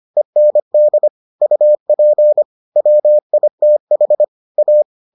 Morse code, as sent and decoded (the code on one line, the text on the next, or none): . -. -.. / ..- .--. / .-- .. - .... / .-
END UP WITH A